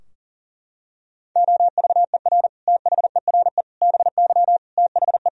Morse code, heard as a keyed tone, about 37 wpm, 700 Hz. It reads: OVER THERE BY THE
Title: Sentences from Top 1000 Words - Review Part 3 - 40wpm